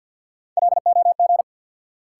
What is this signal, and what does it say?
Morse code, heard as a keyed tone, about 53 wpm, 700 Hz. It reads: LOG
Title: Sentences from Top 1400 Words - New Word Drill - 50wpm